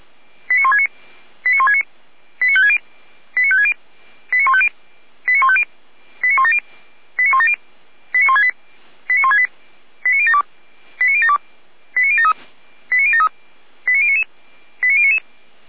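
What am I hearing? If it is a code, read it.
Five-tone selective call (ZVEI), heard as quick runs of five tones, tones of about 70 ms. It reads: 89169 89169 89560 89560 89150 89150 89170 89170 89168 89168 89072 89072 89073 89073 89099 89099
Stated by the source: Five-tone ZVEI selective call received by SDR, from the feuerwehr-orlamuende.org (F85360) audio server.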